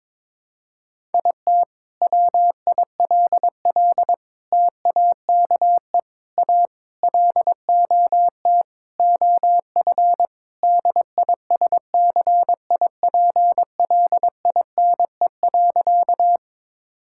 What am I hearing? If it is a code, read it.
Morse code, heard as a keyed tone, about 22 wpm, 700 Hz. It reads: IT WILL TAKE A LOT OF DISCIPLINE.
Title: Sentences from Top 1000 Words - Review Part 6 - 22wpm